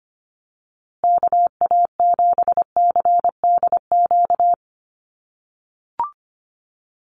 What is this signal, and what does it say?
Morse code, heard as a keyed tone, about 25 wpm, 700 Hz. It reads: KA7CDQ
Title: USA Call Signs Repeated 2x 25wpm